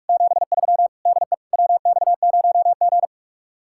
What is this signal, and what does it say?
Morse code, heard as a keyed tone, about 52 wpm, 700 Hz. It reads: 73 DE WX0G
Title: Speed-Racing Common QSO Phrases - 45wpm